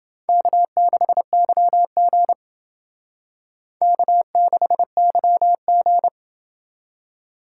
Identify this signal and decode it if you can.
Morse code, heard as a keyed tone, about 28 wpm, 700 Hz. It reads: K6YG K6YG